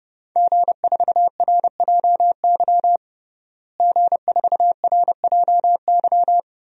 Morse code, heard as a keyed tone, about 30 wpm, 700 Hz. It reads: G4RJY G4RJY
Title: DXCC-100 Call Signs - 5 Characters - Repeated 3x - 30wpm